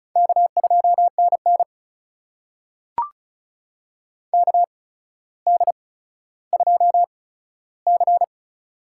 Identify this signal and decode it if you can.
Morse code, heard as a keyed tone, about 36 wpm, spaced as for 15 wpm, 700 Hz. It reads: K2NN KD2C
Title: USA Call Signs - 4 Characters - 8x Character Spacing - 35wpm